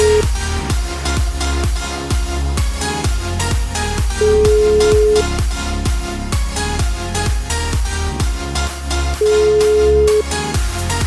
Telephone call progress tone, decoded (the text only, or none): ringback tone